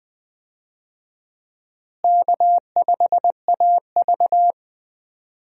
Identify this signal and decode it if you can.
Morse code, heard as a keyed tone, about 20 wpm, 700 Hz. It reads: K5AV
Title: USA Call Signs - 4 Characters - Repeated 3x - 20wpm